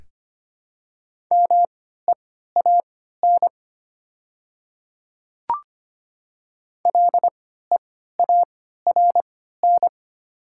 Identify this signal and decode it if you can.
Morse code, heard as a keyed tone, about 25 wpm, spaced as for 14 wpm, 700 Hz. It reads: MEAN LEARN
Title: 3x Character Spacing - Top 300 Words - 25wpm